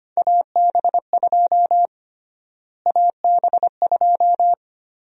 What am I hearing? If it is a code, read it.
Morse code, heard as a keyed tone, about 25 wpm, 700 Hz. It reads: AB2 AB2
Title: USA Call Sign Prefixes - Repeated 2x - 25wpm